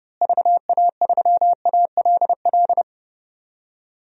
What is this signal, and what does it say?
Morse code, heard as a keyed tone, about 32 wpm, 700 Hz. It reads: VA3ALL
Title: DXCC-100 Call Signs - 6 Characters - Repeated 2x - 30wpm